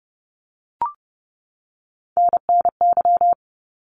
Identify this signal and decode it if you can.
Morse code, heard as a keyed tone, about 30 wpm, 700 Hz. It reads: NNY
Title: ARRL Sections 30wpm